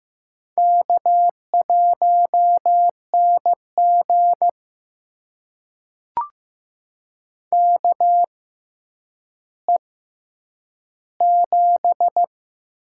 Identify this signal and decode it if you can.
Morse code, heard as a keyed tone, about 15 wpm, spaced as for 9 wpm, 700 Hz. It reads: K1NG KE7
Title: USA Call Signs - 6x Character Spacing - 15wpm